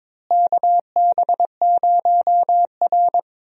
Morse code, heard as a keyed tone, about 22 wpm, 700 Hz. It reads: KB0R
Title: USA Call Signs - 4 Characters - Repeated 3x - 22wpm